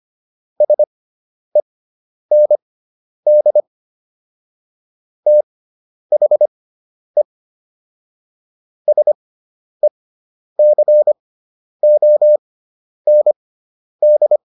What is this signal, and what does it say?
Morse code, heard as a keyed tone, about 25 wpm, spaced as for 10 wpm, 600 Hz. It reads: SEND THE SECOND